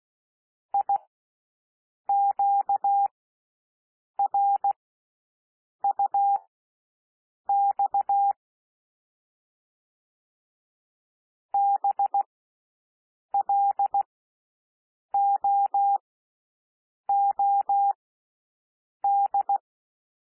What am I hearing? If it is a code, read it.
Morse code, heard as a keyed tone, about 16 wpm, spaced as for 6 wpm, 800 Hz. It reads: IQRUX BLOOD
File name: cwmp3 / L1 SOUND LETTERS 16 wpm